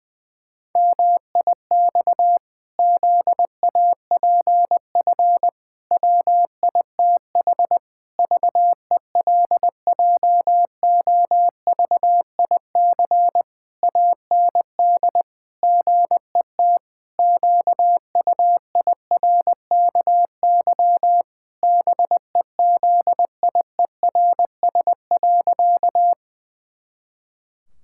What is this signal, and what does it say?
Morse code, heard as a keyed tone, about 20 wpm, 700 Hz. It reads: MIX ZAPF WITH VELJOVIC AND GET QUIRKY BEZIERS.
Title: Pangram Challenge v2 - 20wpm